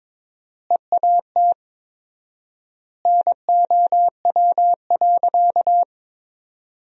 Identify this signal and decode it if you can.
Morse code, heard as a keyed tone, about 22 wpm, 700 Hz. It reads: EAT NOW.
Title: 4x Word-Spacing - Sentences from Top 300 Words - 22wpm